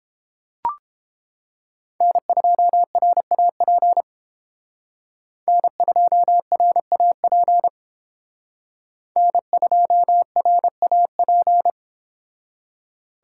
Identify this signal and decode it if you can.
Morse code, heard as a keyed tone, about 31 wpm, 700 Hz. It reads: N2RAP N2RAP N2RAP